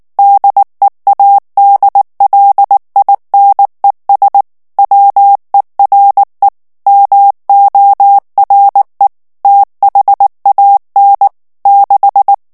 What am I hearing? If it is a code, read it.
Morse code, heard as a keyed tone, about 19 wpm, 800 Hz. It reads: DEADLINES WERE MORE THAN 6
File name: ANGLAIS texte clair 3 vitesse 18 mots minute